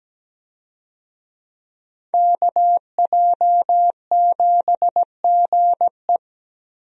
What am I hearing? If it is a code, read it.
Morse code, heard as a keyed tone, about 17 wpm, 700 Hz. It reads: KJ7GE
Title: USA Call Signs - 5 Characters - Repeated 3x - 17wpm